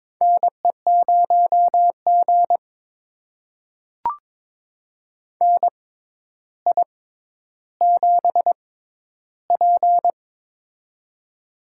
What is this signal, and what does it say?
Morse code, heard as a keyed tone, about 22 wpm, spaced as for 11 wpm, 700 Hz. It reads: NE0G NI7P
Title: USA Call Signs - 4 Characters - 6x Character Spacing - 22wpm